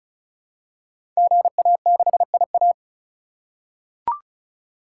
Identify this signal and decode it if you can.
Morse code, heard as a keyed tone, about 35 wpm, 700 Hz. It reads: GA6IA
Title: Code Groups - 5 Letter:Numbers - 35wpm